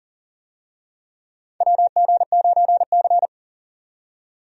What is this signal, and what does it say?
Morse code, heard as a keyed tone, about 40 wpm, 700 Hz. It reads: WG9C
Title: USA Call Signs - 4 Characters - 40wpm